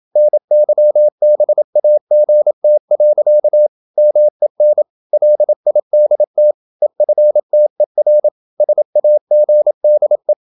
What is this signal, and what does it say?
Morse code, heard as a keyed tone, about 27 wpm, 600 Hz. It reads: NYBAGT. MEN LIDT EFTER SAGDE
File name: Ferie QRQ 27wpm